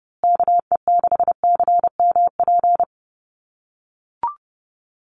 Morse code, heard as a keyed tone, about 30 wpm, 700 Hz. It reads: KE6CMP
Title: USA Call Signs Repeated 3x 30wpm